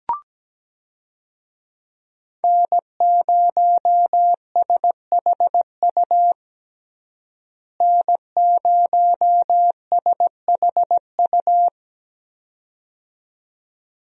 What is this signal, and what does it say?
Morse code, heard as a keyed tone, about 17 wpm, 700 Hz. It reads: N0SHU N0SHU
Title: USA Call Signs - 5 Characters - Repeated 2x - 17wpm